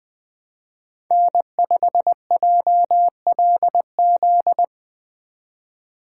Morse code, heard as a keyed tone, about 23 wpm, 700 Hz. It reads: N5JLZ
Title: Speed Racing - USA 1x3 Call Signs 20wpm